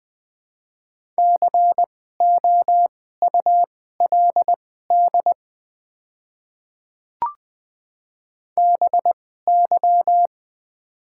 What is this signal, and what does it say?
Morse code, heard as a keyed tone, about 20 wpm, spaced as for 14 wpm, 700 Hz. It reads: COULD BY
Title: 2x Character Spacing - Top 100 Words - 20wpm